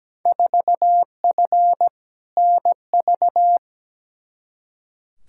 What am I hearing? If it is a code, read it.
Morse code, heard as a keyed tone, about 17 wpm, 700 Hz. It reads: 4F NV
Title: Field Day Exchanges Encore - 17wpm